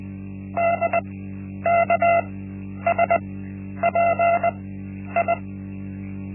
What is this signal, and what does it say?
Morse code, heard as a keyed tone, about 20 wpm, spaced as for 10 wpm, 680 Hz. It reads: DKSPI
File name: Linksmakalnis-1